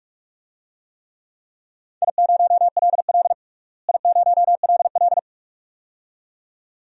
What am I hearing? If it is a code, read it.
Morse code, heard as a keyed tone, about 46 wpm, 700 Hz. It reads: I0LL I0LL
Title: DXCC-100 Call Signs - 4 Characters - Repeated 2x - 45wpm